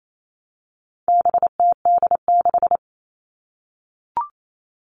Morse code, heard as a keyed tone, about 28 wpm, 700 Hz. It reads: BTD6
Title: All Cut Numbers - 4-Digit 28wpm